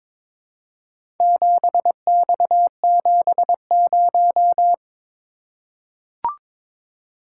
Morse code, heard as a keyed tone, about 22 wpm, 700 Hz. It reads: 7X70
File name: practice / 22wpm / Four Letter-Number Combinations - 22wpm